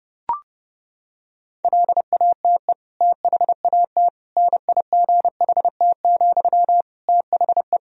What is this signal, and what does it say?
Morse code, heard as a keyed tone, about 30 wpm, 700 Hz. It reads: LATE THAT NIGHT, THE
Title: Sentences from Top 1000 Words - Review Part 4 - 30wpm